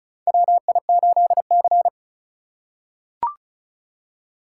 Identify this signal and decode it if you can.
Morse code, heard as a keyed tone, about 36 wpm, 700 Hz. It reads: WI8C
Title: USA Call Signs - 4 Characters - 3x Character Spacing - 35wpm